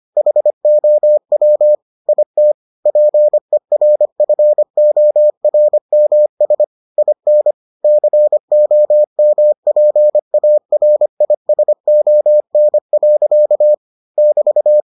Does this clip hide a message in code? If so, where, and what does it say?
Yes: Morse code from 0.2 to 14.8 s, heard as a keyed tone, about 25 wpm, 600 Hz. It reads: HOW IT PERFORMS IN COMPARISON. <BT>